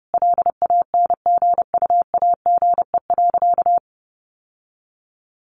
Morse code, heard as a keyed tone, about 30 wpm, 700 Hz. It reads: LANGUAGE.
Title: Sentences from Top 1300 Words 30wpm